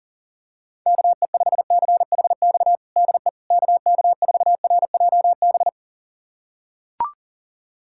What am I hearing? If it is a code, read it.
Morse code, heard as a keyed tone, about 40 wpm, 700 Hz. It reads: KE5CHX DE KK4RJB
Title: USA Callsign DE Callsign - 6 Characters - 40wpm